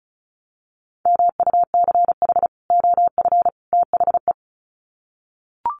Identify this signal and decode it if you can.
Morse code, heard as a keyed tone, about 34 wpm, 700 Hz. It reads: MUCH OF THE
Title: Sets of 3 Words 35wpm